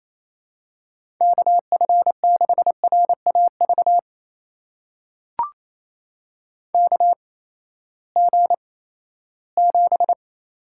Morse code, heard as a keyed tone, about 28 wpm, spaced as for 16 wpm, 700 Hz. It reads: KF6RAV KG7
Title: USA Call Signs - 6 Characters - 8x Character Spacing - 28wpm